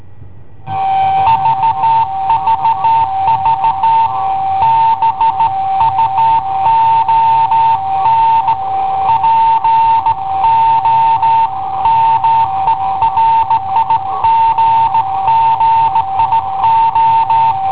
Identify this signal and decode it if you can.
Morse code, heard as a keyed tone, about 16 wpm, 900 Hz. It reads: VVV BUON POMERIGGIO
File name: i7ohp / cassetta i7ohp B